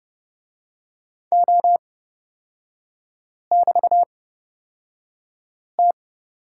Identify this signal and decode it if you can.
Morse code, heard as a keyed tone, about 29 wpm, spaced as for 5 wpm, 700 Hz. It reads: O<BT>T
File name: Lesson 36e - BT - Three Characters 30wpm